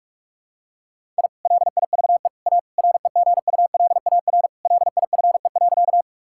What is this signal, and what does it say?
Morse code, heard as a keyed tone, about 47 wpm, 700 Hz. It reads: I LIVE A REGULAR LIFE.
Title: Sentences from Top 1000 Words - Review Part 6 - 45wpm